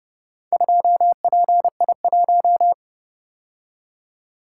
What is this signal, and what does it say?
Morse code, heard as a keyed tone, about 29 wpm, 700 Hz. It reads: 2PI1